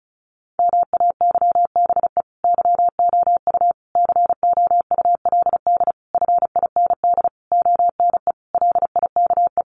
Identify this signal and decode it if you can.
Morse code, heard as a keyed tone, about 34 wpm, 700 Hz. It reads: MAYBE YOU COULD FIND ONE LIKE
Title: Sentences from Top 1400 Words 35wpm